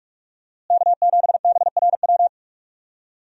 Morse code, heard as a keyed tone, about 44 wpm, 700 Hz. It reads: K7BRW
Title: USA Call Signs - 5 Characters - Repeated 3x - 45wpm